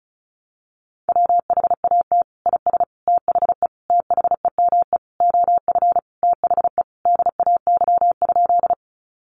Morse code, heard as a keyed tone, about 34 wpm, 700 Hz. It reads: WHAT IS THE THEME OF THE DAY?
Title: Sentences from Top 1100 Words 35wpm